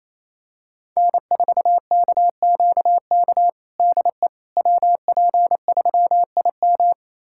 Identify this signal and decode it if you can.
Morse code, heard as a keyed tone, about 28 wpm, 700 Hz. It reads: N4KQK DE WP3IM
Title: USA Callsign DE Callsign - 5 Characters - 28wpm